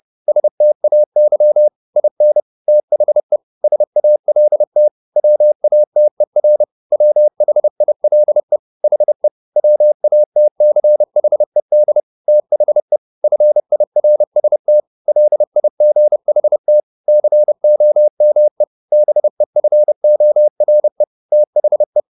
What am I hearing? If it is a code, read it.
Morse code, heard as a keyed tone, about 30 wpm, 600 Hz. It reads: STAY IN THE SALT WATER WHILE HE WATCHED THE FIRST LIGHT COME BEFORE THE